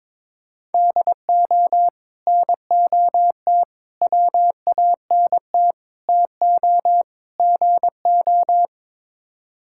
Morse code, heard as a keyed tone, about 22 wpm, 700 Hz. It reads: DO NOT WANT TO GO
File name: practice / 22wpm / Build A Phrase - 5-Word Phrases - 22wpm